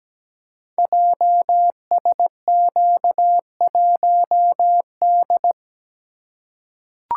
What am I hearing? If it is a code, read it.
Morse code, heard as a keyed tone, about 17 wpm, 700 Hz. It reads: JSQ1D
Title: Code Groups - 5 Letter:Numbers - 17wpm